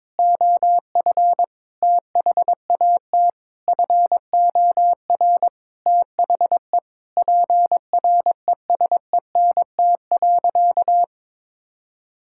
Morse code, heard as a keyed tone, about 21 wpm, 700 Hz. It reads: OF THAT FOR THE PRESENT.
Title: Sentences from Top 1000 Words - Review Part 5 - 22wpm